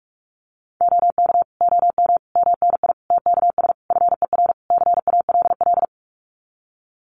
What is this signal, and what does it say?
Morse code, heard as a keyed tone, about 45 wpm, 700 Hz. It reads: OK OM MNI TKS FER CALL
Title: Common QSO Phrases 45wpm